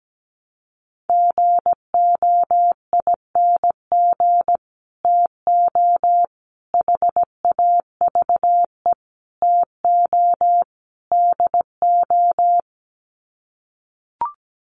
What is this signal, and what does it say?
Morse code, heard as a keyed tone, about 17 wpm, 690 Hz. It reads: GOING TO HAVE TO DO
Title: Sets of 5 Words Encore 17wpm